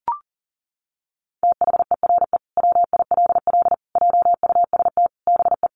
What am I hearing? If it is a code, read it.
Morse code, heard as a keyed tone, about 40 wpm, 700 Hz. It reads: THERE WILL JUST BE
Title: Sentences from Top 200 Words 40wpm